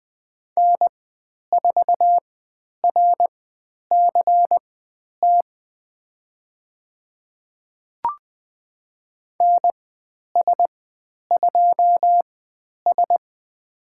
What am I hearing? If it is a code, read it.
Morse code, heard as a keyed tone, about 20 wpm, spaced as for 10 wpm, 700 Hz. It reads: N4RCT NS2S
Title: 095 Lesson 16g - Letter c - Call Signs 20wpm